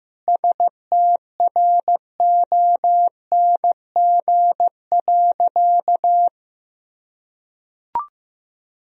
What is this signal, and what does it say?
Morse code, heard as a keyed tone, about 15 wpm, 700 Hz. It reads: STRONG.
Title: Sentences from Top 1000 Words - Review Part 2 - 15wpm